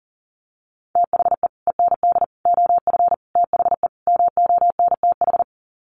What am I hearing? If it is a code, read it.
Morse code, heard as a keyed tone, about 40 wpm, 700 Hz. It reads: THE END OF THE MONTH
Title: Sets of 5 Words Encore 40wpm